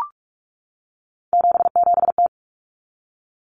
Morse code, heard as a keyed tone, about 44 wpm, 700 Hz. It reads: 77T
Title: Common Cut Numbers - 3-Digit 45wpm